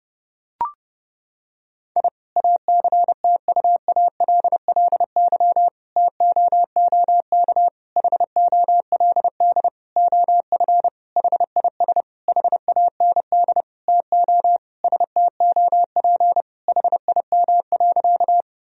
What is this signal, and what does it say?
Morse code, heard as a keyed tone, about 30 wpm, 700 Hz. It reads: I ACTUALLY TOOK HOLD OF HIS HAND TO STOP HIM.
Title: Sentences from Top 1000 Words - Review Part 4 - 30wpm